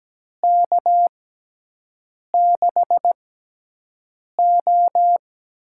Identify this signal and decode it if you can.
Morse code, heard as a keyed tone, about 17 wpm, spaced as for 6 wpm, 700 Hz. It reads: K6O
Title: USA Call Signs - 6 Characters - 6x Character Spacing - 17wpm